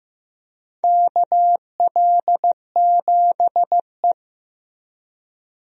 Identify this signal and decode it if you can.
Morse code, heard as a keyed tone, about 15 wpm, 700 Hz. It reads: KL7E
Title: DXCC-100 Call Signs - 4 Characters - 15wpm